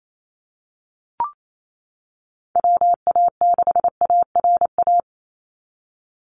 Morse code, heard as a keyed tone, about 28 wpm, 700 Hz. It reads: WA6ARA
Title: NA SOTA Chaser Callsigns 28wpm